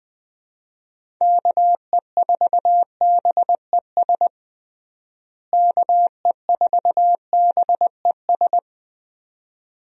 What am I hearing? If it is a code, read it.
Morse code, heard as a keyed tone, about 20 wpm, 700 Hz. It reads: KE4BES KE4BES
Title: USA Call Signs - 6 Characters - Repeated 2x - 20wpm